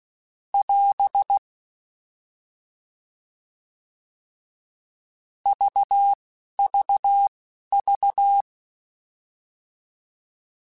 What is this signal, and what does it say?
Morse code, heard as a keyed tone, about 16 wpm, spaced as for 11 wpm, 800 Hz. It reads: <AS> VVV
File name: Alphabetical / Alphabetical mp3s / Es @10